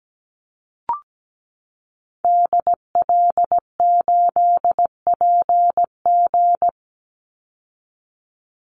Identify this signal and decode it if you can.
Morse code, heard as a keyed tone, about 17 wpm, 700 Hz. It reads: DL8PG
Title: Call Signs Repeated 2x - 17wpm